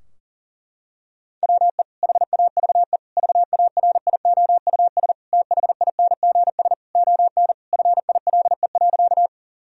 Morse code, heard as a keyed tone, about 37 wpm, 700 Hz. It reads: WE HAVE VARIOUS THINGS ON FILE.
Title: Sentences from Top 1000 Words - Review Part 4 - 40wpm